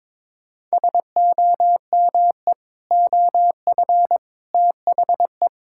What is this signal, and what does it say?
Morse code, heard as a keyed tone, about 22 wpm, 700 Hz. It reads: SOME OF THE
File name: Sentences with 7-Word Phrases - 22wpm